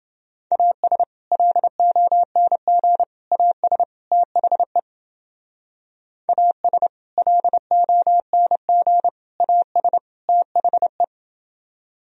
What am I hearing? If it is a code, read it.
Morse code, heard as a keyed tone, about 29 wpm, 700 Hz. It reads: AS LONG AS THE AS LONG AS THE